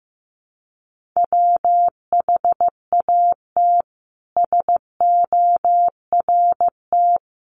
Morse code, heard as a keyed tone, about 15 wpm, 700 Hz. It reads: WHAT SORT
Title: Sentences from Top 1300 Words 15wpm